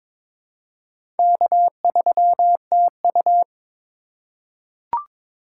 Morse code, heard as a keyed tone, about 22 wpm, 700 Hz. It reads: K3TU
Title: USA Call Signs - 4 Characters - 2x Character Spacing - 22wpm